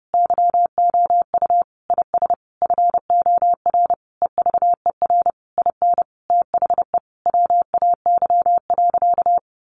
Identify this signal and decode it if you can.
Morse code, heard as a keyed tone, about 30 wpm, 700 Hz. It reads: YOU IS FOR EVER IN THE WAY.
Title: Sentences from Top 400 Words 30wpm